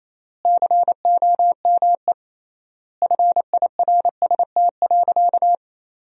Morse code, heard as a keyed tone, about 27 wpm, 700 Hz. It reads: COME FIRST.
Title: 3x Word-Spacing - Sentences from Top 300 Words - 28wpm